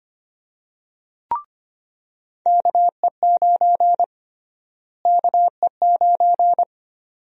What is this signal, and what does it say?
Morse code, heard as a keyed tone, about 25 wpm, 700 Hz. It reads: KE9 KE9
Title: USA Call Sign Prefixes - Repeated 2x - 25wpm